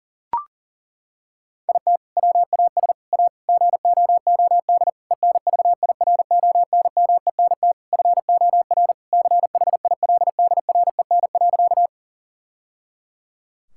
Morse code, heard as a keyed tone, about 40 wpm, 700 Hz. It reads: IT WAS A GOOD ENVIRONMENT FOR CHILDREN.
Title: Sentences from Top 1000 Words - Review Part 5 - 40wpm